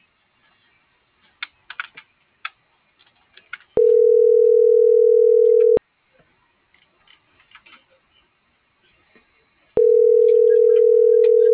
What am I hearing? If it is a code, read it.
Ringback tone (North America), 440 + 480 Hz.
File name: CENTRAL MABE / out david.toribio